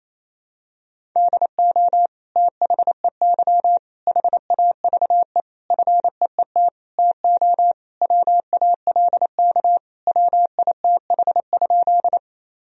Morse code, heard as a keyed tone, about 28 wpm, 700 Hz. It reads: DO THEY HAVE FEET TO WALK WITH?
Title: Sentences from Top 1000 Words - Review Part 1 - 28wpm